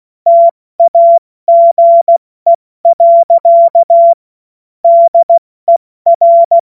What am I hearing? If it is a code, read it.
Morse code, heard as a keyed tone, about 16 wpm, 680 Hz. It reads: TAGE. DER
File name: Morsekurs / Witze / Witz 12 März 2024 70 F80